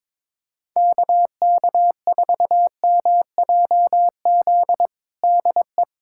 Morse code, heard as a keyed tone, about 22 wpm, 700 Hz. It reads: KK4MJZ DE
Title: USA Callsign DE Callsign - 6 Characters - 22wpm